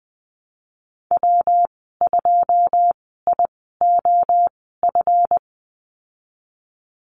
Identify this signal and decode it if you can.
Morse code, heard as a keyed tone, about 20 wpm, spaced as for 15 wpm, 700 Hz. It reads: W2IOF
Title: 161 Lesson 26g - Letter g - Call Signs 20wpm